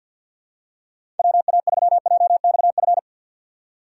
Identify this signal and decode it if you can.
Morse code, heard as a keyed tone, about 51 wpm, 700 Hz. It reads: WA3JXF
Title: USA Call Signs - 6 Characters - 8x Character Spacing - 50wpm